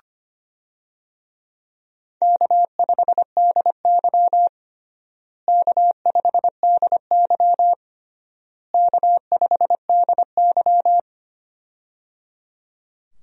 Morse code, heard as a keyed tone, about 25 wpm, 700 Hz. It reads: K5DY K5DY K5DY
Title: USA Call Signs - 4 Characters - Repeated 3x - 25wpm